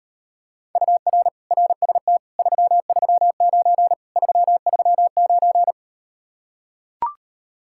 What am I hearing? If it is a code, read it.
Morse code, heard as a keyed tone, about 38 wpm, 700 Hz. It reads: UR RST 339 339